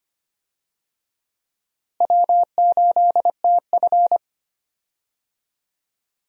Morse code, heard as a keyed tone, about 25 wpm, 700 Hz. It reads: W8TF
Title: USA Call Signs - 4 Characters - Repeated 3x - 25wpm